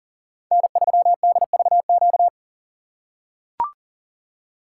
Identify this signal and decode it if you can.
Morse code, heard as a keyed tone, about 40 wpm, 700 Hz. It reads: N3DVQ
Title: USA Call Signs - 5 Characters - 2x Character Spacing - 40wpm